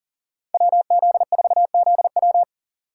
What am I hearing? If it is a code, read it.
Morse code, heard as a keyed tone, about 40 wpm, 700 Hz. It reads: WZ4ZW
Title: USA Call Signs - 5 Characters - 3x Character Spacing - 40wpm